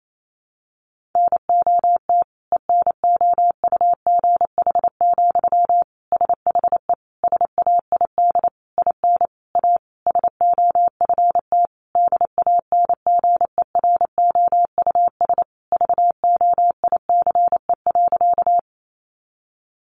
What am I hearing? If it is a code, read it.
Morse code, heard as a keyed tone, about 28 wpm, 700 Hz. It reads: NOT ENOUGH, SHE SAID IN A SOFT DANGEROUS VOICE.
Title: Sentences from Top 1500 Words 28wpm